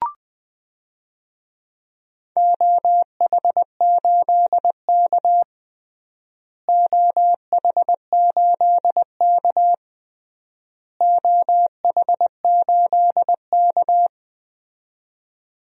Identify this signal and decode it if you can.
Morse code, heard as a keyed tone, about 21 wpm, 700 Hz. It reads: OH8K OH8K OH8K
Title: DXCC-100 Call Signs - 4 Characters - Repeated 3x - 20wpm